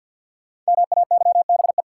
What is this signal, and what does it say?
Morse code, heard as a keyed tone, about 50 wpm, 700 Hz. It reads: MAYBE
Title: Sentences from Top 1000 Words - New Word Drill - 50wpm